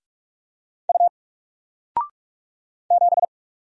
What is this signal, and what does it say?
Morse code, heard as a keyed tone, about 45 wpm, 700 Hz. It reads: U 7
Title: Single Letter-Number Alt 45wpm